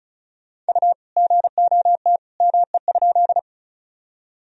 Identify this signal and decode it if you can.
Morse code, heard as a keyed tone, about 34 wpm, 700 Hz. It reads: U GOT ME?
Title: Sentences from Top 300 Words and Abbreviations 35wpm